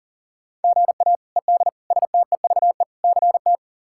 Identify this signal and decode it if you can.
Morse code, heard as a keyed tone, about 40 wpm, 700 Hz. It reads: GA ED STEVE CT
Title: K1USN SST QSO Practice - 40wpm